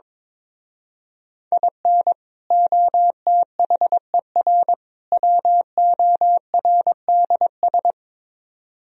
Morse code, heard as a keyed tone, about 22 wpm, 700 Hz. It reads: IN OTHER WORDS
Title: Speed-Racing - Sets of 3 Words 22wpm